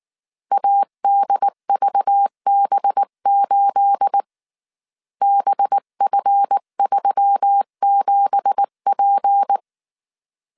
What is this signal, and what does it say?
Morse code, heard as a keyed tone, about 19 wpm, 800 Hz. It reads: ADVB8 BF37P
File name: CW / DET / 1 9 1500C S B 18wpm 2551